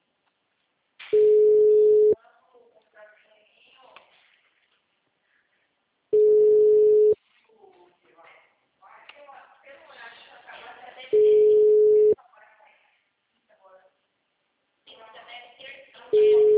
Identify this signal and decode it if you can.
Ringback tone (Europe, 425 Hz), with speech.